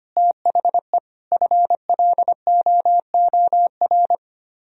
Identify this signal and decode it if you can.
Morse code, heard as a keyed tone, about 33 wpm, 700 Hz. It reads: THE FLOOR